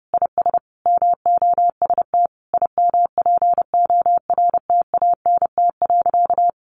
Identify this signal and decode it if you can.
Morse code, heard as a keyed tone, about 30 wpm, 700 Hz. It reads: IS MOST IMPORTANT.
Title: Sentences from Top 1200 Words 30wpm